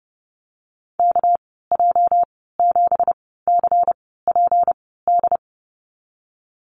Morse code, heard as a keyed tone, about 30 wpm, spaced as for 17 wpm, 700 Hz. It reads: KJ7CPD
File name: Lesson 37g - Letter x - Call Signs 30wpm